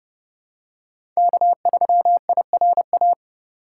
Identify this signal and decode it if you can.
Morse code, heard as a keyed tone, about 30 wpm, 700 Hz. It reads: K3IRA
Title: USA Call Signs - 5 Characters - 8x Character Spacing - 30wpm